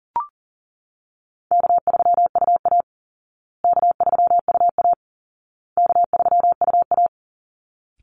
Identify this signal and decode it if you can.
Morse code, heard as a keyed tone, about 37 wpm, 700 Hz. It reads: K3UA K3UA K3UA
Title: Easy Call Signs Repeated 3x - 40wpm